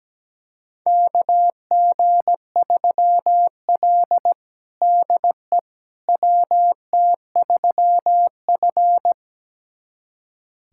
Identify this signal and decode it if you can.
Morse code, heard as a keyed tone, about 17 wpm, 700 Hz. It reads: KG3L DE WT3F
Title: USA Callsign DE Callsign - 4 Characters - 17wpm